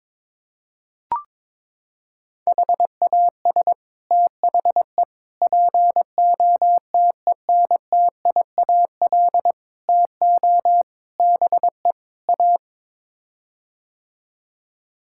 Morse code, heard as a keyed tone, about 22 wpm, 700 Hz. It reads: HAS THE POTENTIAL TO BE A
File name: practice / 22wpm / Sets of 6 Words - 22wpm